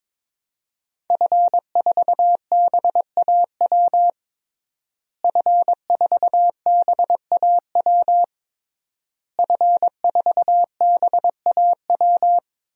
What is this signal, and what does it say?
Morse code, heard as a keyed tone, about 22 wpm, 700 Hz. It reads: F4BAW F4BAW F4BAW
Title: DXCC-100 Call Signs - 5 Characters - Repeated 3x - 22wpm